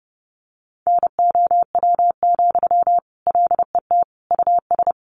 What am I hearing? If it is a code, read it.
Morse code, heard as a keyed tone, about 30 wpm, 700 Hz. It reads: NOW, LET US
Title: Sentences from Top 1500 Words 30wpm